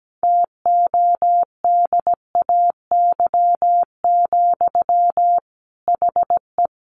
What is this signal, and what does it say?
Morse code, heard as a keyed tone, about 17 wpm, 700 Hz. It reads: TODAY, HE
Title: Sentences from Top 600 Words 17wpm